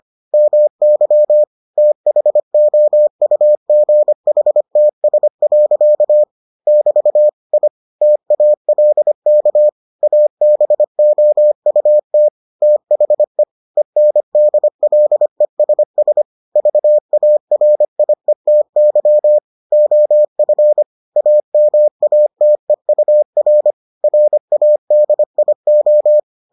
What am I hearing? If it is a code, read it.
Morse code, heard as a keyed tone, about 25 wpm, 600 Hz. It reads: MY THOUGHTS. <BT> I TALK ABOUT THE ENDLESS VARIETY OF AMATEUR RADIO